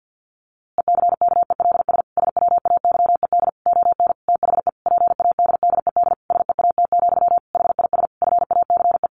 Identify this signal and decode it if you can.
Morse code, heard as a keyed tone, about 50 wpm, 700 Hz. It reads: ECKELS SWAYED ON THE PADDED SEAT, HIS FACE
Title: A Sound Of Thunder 50wpm